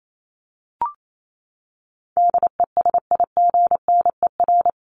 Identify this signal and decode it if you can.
Morse code, heard as a keyed tone, about 28 wpm, 700 Hz. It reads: DESIGNER
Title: Top 1501-1750 Words 28wpm